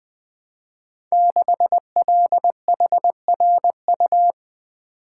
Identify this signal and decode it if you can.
Morse code, heard as a keyed tone, about 20 wpm, 700 Hz. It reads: 6LHRU
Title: Code Groups - 5 Letter:Numbers - 20wpm